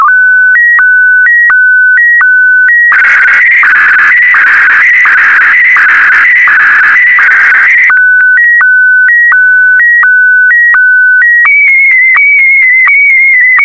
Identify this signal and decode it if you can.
SSTV (slow-scan television), PD 90.